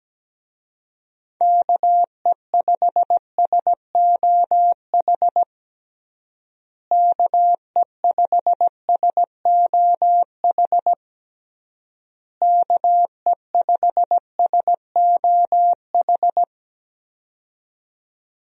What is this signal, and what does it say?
Morse code, heard as a keyed tone, about 17 wpm, 700 Hz. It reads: KE5SOH KE5SOH KE5SOH
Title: USA Call Signs - 6 Characters - Repeated 3x - 17wpm